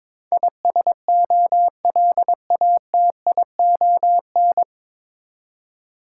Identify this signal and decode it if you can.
Morse code, heard as a keyed tone, about 22 wpm, 700 Hz. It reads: ISOLATION
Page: Morse Code Ninja practice 22wpm